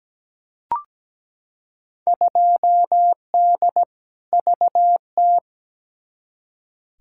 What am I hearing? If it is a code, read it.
Morse code, heard as a keyed tone, about 17 wpm, 700 Hz. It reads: 2D VT
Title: Field Day Exchanges Encore - 17wpm